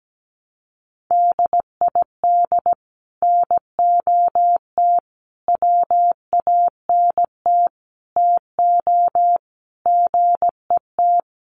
Morse code, heard as a keyed tone, about 17 wpm, 700 Hz. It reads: DID NOT WANT TO GET
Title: Sets of 5 Words Encore2 17wpm